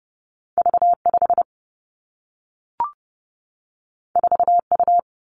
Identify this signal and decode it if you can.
Morse code, heard as a keyed tone, about 30 wpm, 700 Hz. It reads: V 5 4 U
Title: Two Challenging Letter-Number Combinations 30wpm